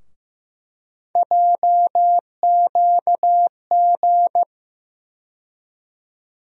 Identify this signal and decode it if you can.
Morse code, heard as a keyed tone, about 15 wpm, 700 Hz. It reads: JQG
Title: USA Call Sign Postfixes - 3 Letters 15wpm